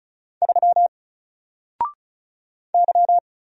Morse code, heard as a keyed tone, about 36 wpm, 700 Hz. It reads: 3 Y
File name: Single-Letter-Number-35wpm